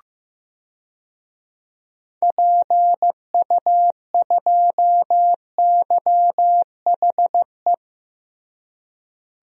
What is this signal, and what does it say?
Morse code, heard as a keyed tone, about 15 wpm, 700 Hz. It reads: PU2YHE
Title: DXCC-100 Call Signs - 6 Characters - Repeated 2x - 15wpm